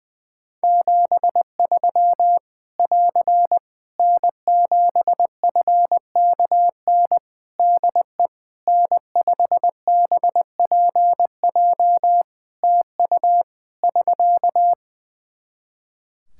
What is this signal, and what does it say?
Morse code, heard as a keyed tone, about 20 wpm, 700 Hz. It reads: 73 <AR> N7FKN DE N5BPJ TU <SK>
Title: Practice QSO - Exchange 4 - Ending - 20wpm